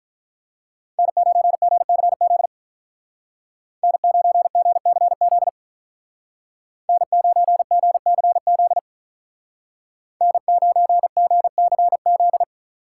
Morse code, heard as a keyed tone, about 46 wpm, 700 Hz. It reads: N9GCZ N9GCZ N9GCZ N9GCZ